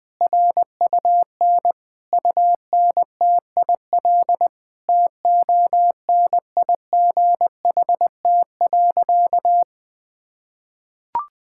Morse code, heard as a keyed tone, about 21 wpm, 700 Hz. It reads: RUN UNTIL TONIGHT.
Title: Sentences from Top 1000 Words - Review Part 5 - 20wpm